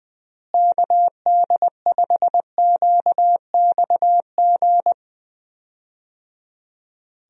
Morse code, heard as a keyed tone, about 20 wpm, 700 Hz. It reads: KD5QXG
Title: USA Call Signs - 6 Characters - Repeated 3x - 20wpm